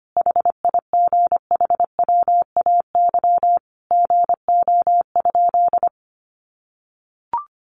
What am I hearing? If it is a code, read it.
Morse code, heard as a keyed tone, about 25 wpm, 700 Hz. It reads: HIGHWAY GO?
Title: Sentences from Top 1400 Words 25wpm